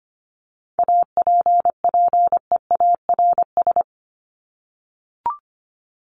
Morse code, heard as a keyed tone, about 25 wpm, 700 Hz. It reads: APPEARS
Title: ES:S Suffix - Top 100 Words with Base Words 25wpm